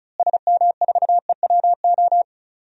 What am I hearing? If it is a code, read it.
Morse code, heard as a keyed tone, about 36 wpm, 700 Hz. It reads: SM4EWO
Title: DXCC-100 Call Signs - 6 Characters - Repeated 3x - 35wpm